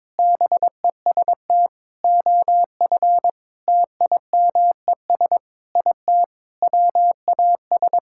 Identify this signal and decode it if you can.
Morse code, heard as a keyed tone, about 22 wpm, 700 Hz. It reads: BEST OF TIMES IT WAS
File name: practice / 22wpm / Sets of 8 Words - 22wpm